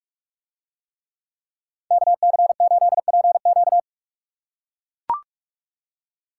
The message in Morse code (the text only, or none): KC8PQ